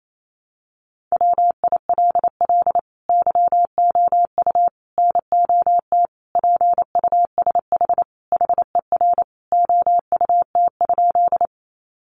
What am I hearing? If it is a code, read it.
Morse code, heard as a keyed tone, about 28 wpm, 700 Hz. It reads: WILL YOU NOT PUSH HER OUT?
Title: Sentences from Top 1300 Words 28wpm